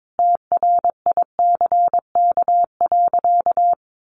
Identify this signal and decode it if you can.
Morse code, heard as a keyed tone, about 22 wpm, 700 Hz. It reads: TRICK.
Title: Sentences from Top 1500 Words 22wpm